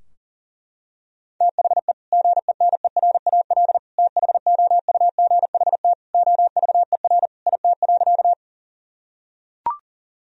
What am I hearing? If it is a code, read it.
Morse code, heard as a keyed tone, about 40 wpm, 700 Hz. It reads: THE GENERAL THOUGHT OVER IT.
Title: Sentences from Top 1000 Words - Review Part 3 - 40wpm